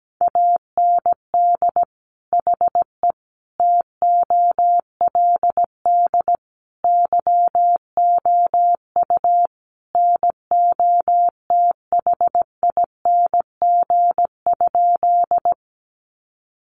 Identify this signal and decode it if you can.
Morse code, heard as a keyed tone, about 17 wpm, 700 Hz. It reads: AND HE TOLD YOU NOTHING?
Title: Sentences from Top 500 Words 17wpm